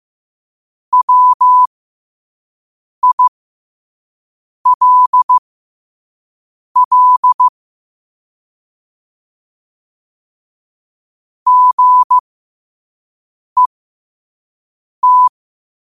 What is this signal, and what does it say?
Morse code, heard as a keyed tone, about 15 wpm, spaced as for 5 wpm, 1000 Hz. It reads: WILL GET
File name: qotdfiles / 5wpm